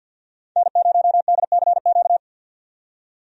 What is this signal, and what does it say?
Morse code, heard as a keyed tone, about 49 wpm, 700 Hz. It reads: N0DCQ
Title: USA Call Signs - 5 Characters - 50wpm